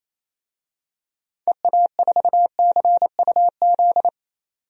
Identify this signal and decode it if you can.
Morse code, heard as a keyed tone, about 28 wpm, 700 Hz. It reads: EA4CUZ
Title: DXCC-100 Call Signs - 6 Characters - 28wpm